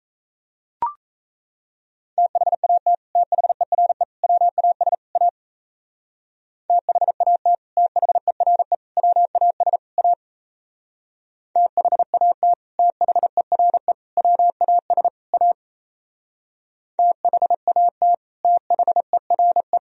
Morse code, heard as a keyed tone, about 37 wpm, 700 Hz. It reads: THAT THERE WAS A THAT THERE WAS A THAT THERE WAS A THAT THERE